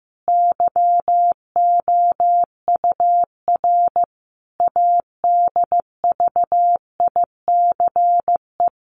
Morse code, heard as a keyed tone, about 15 wpm, 700 Hz. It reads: YOUR ADVICE
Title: Sentences from Top 1100 Words 15wpm